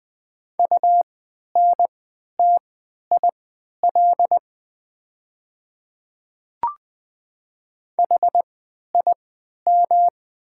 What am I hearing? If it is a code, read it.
Morse code, heard as a keyed tone, about 20 wpm, spaced as for 11 wpm, 700 Hz. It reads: UNTIL HIM
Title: 3x Character Spacing - Top 300 Words - 20wpm